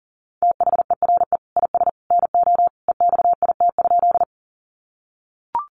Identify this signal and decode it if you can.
Morse code, heard as a keyed tone, about 40 wpm, 700 Hz. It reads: THERE IS NO EXIT?
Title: Sentences from Top 1100 Words 40wpm